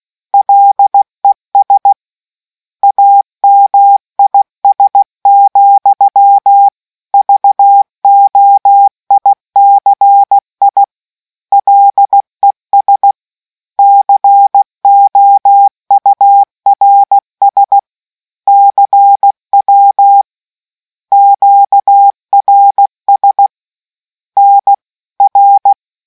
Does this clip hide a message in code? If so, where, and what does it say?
Yes: Morse code from 0.3 to 25.7 s, heard as a keyed tone, about 16 wpm, 800 Hz. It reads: LES AMIS, VOICI LES COURS CW QRS NR